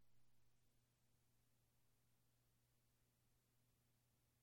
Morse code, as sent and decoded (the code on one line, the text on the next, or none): none